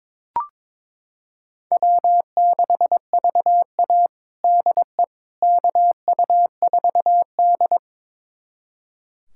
Morse code, heard as a keyed tone, about 21 wpm, 700 Hz. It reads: W6VA DE KU4D
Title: USA Callsign DE Callsign - 4 Characters - 22wpm